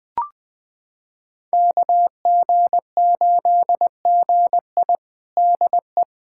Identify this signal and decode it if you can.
Morse code, heard as a keyed tone, about 21 wpm, 700 Hz. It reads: KG8GI DE
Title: USA Callsign DE Callsign 20wpm